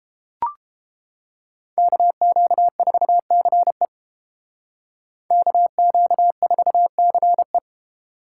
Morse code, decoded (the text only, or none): KQ4CE KQ4CE